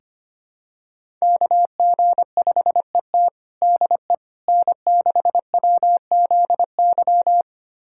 Morse code, heard as a keyed tone, about 25 wpm, 700 Hz. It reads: KG5ET DE N6WZY
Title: USA Callsign DE Callsign - 5 Characters - 25wpm